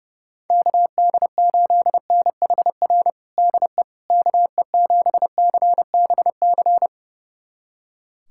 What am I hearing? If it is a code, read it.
Morse code, heard as a keyed tone, about 32 wpm, 700 Hz. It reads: KD8NHR DE KE7CBC
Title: USA Callsign DE Callsign 30wpm